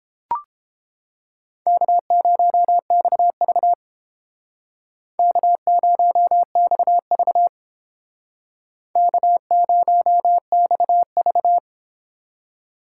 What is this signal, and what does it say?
Morse code, heard as a keyed tone, about 30 wpm, 700 Hz. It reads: K0XV K0XV K0XV